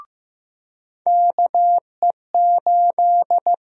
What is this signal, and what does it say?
Morse code, heard as a keyed tone, about 15 wpm, 700 Hz. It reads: KE8
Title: USA Call Sign Prefixes - 15wpm